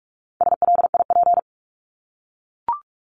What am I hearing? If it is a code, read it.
Morse code, heard as a keyed tone, about 44 wpm, 700 Hz. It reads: SLIP
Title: Top 1751-2000 Words 45wpm